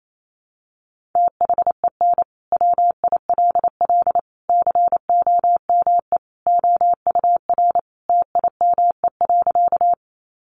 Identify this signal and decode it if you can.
Morse code, heard as a keyed tone, about 28 wpm, 700 Hz. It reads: THEN WILL COME OUR TIME.
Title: Sentences from Top 200 Words 28wpm